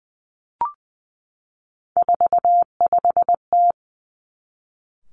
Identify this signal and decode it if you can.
Morse code, heard as a keyed tone, about 20 wpm, 700 Hz. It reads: 45T
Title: Three Challenging Letter-Number Combinations 20wpm